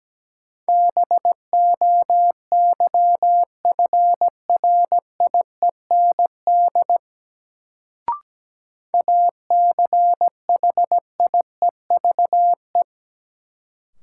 Morse code, heard as a keyed tone, about 17 wpm, 700 Hz. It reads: BOYFRIEND ACHIEVE
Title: Sentences from Top 1000 Words - Review Part 5 - New Word Drill - 17wpm